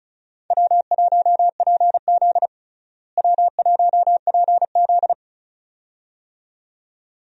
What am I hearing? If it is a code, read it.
Morse code, heard as a keyed tone, about 34 wpm, 700 Hz. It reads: W1PZ W1PZ
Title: USA Call Signs - 4 Characters - Repeated 2x - 35wpm